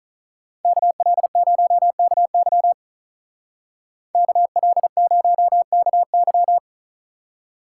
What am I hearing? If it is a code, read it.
Morse code, heard as a keyed tone, about 43 wpm, 700 Hz. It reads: KL0KY KL0KY
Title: Speed Racing - USA 2x2 Call Signs 35wpm